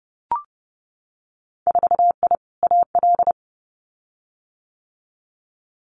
Morse code, heard as a keyed tone, about 30 wpm, 700 Hz. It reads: 4I AL
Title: Winter Field Day Exchanges Encore 30wpm